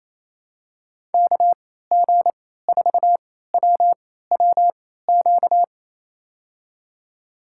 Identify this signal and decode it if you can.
Morse code, heard as a keyed tone, about 27 wpm, spaced as for 16 wpm, 700 Hz. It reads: KG4WWQ
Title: USA Call Signs - 3x Character Spacing - 28wpm